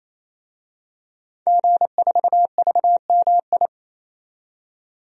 Morse code, heard as a keyed tone, about 28 wpm, 700 Hz. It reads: G4VMI
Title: DXCC-100 Call Signs - 5 Characters - 28wpm